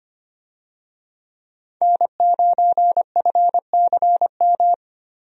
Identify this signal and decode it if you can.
Morse code, heard as a keyed tone, about 25 wpm, 700 Hz. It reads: N9FCM
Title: USA Call Signs - 5 Characters - Repeated 2x - 25wpm